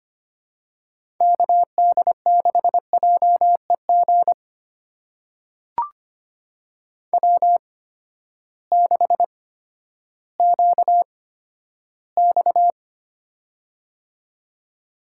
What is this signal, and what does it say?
Morse code, heard as a keyed tone, about 25 wpm, spaced as for 13 wpm, 700 Hz. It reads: KD6JEG W6QX
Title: USA Call Signs - 8x Character Spacing - 25wpm